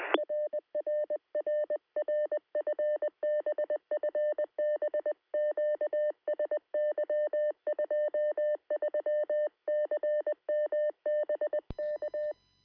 Morse code, heard as a keyed tone, about 20 wpm, 600 Hz. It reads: RRRR FB FB QSY 23CM BK
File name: sm7lcb / 2114z oh3lwp 13cm